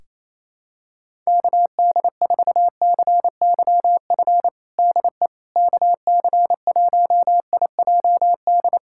Morse code, heard as a keyed tone, about 27 wpm, 700 Hz. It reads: KD4CYF DE KC1IJD
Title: USA Callsign DE Callsign - 6 Characters - 28wpm